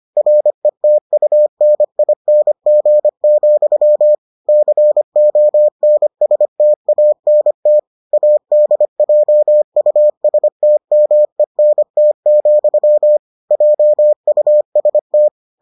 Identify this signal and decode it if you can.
Morse code, heard as a keyed tone, about 25 wpm, 600 Hz. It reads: RETUNING, CONSTANT ADJUSTMENT, JUST